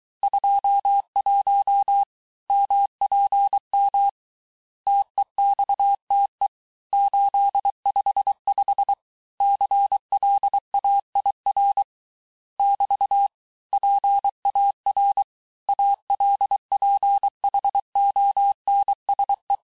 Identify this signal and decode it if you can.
Morse code, heard as a keyed tone, about 23 wpm, 800 Hz. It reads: 21 MPM TEXTE 855 CLAIR <BT> PAR ALPHONSE
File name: cours-du-mercredi / audios / COURS-855--texte-clair-vitesse-22-mpm